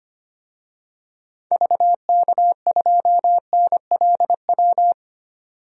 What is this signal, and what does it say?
Morse code, heard as a keyed tone, about 25 wpm, 700 Hz. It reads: VK2NLW
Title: DXCC-100 Call Signs - 6 Characters - 25wpm